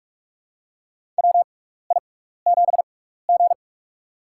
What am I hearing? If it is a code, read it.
Morse code, heard as a keyed tone, about 44 wpm, spaced as for 15 wpm, 700 Hz. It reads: WI7G
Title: USA Call Signs - 6x Character Spacing - 45wpm